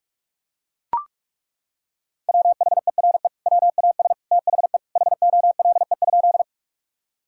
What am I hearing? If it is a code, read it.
Morse code, heard as a keyed tone, about 42 wpm, 700 Hz. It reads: WHERE WAS THE HOLE?
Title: Sentences from Top 1600 Words 45wpm